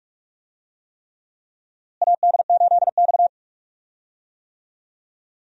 Morse code, heard as a keyed tone, about 43 wpm, 700 Hz. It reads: AD8X
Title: USA Call Signs - 4 Characters - Repeated 3x - 45wpm